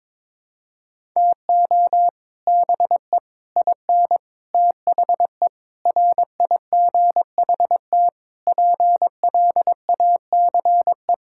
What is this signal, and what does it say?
Morse code, heard as a keyed tone, about 21 wpm, 700 Hz. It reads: TO BE IN THE RIGHT PLACE
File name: Sets of 9 Words - 22wpm